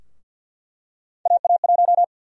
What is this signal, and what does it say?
Morse code, heard as a keyed tone, about 47 wpm, 700 Hz. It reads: AA1
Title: USA Call Sign Prefixes - Repeated 2x - 50wpm